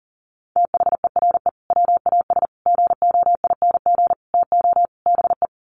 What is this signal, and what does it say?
Morse code, heard as a keyed tone, about 40 wpm, 700 Hz. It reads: THERE WAS GOING TO BE
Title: Sets of 5 Words Encore2 40wpm